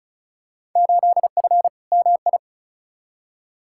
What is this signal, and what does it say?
Morse code, heard as a keyed tone, about 36 wpm, 700 Hz. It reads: 8F MI
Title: Field Day Exchanges Encore - 35wpm